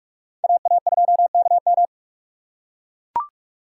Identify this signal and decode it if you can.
Morse code, heard as a keyed tone, about 44 wpm, 700 Hz. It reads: AA2KM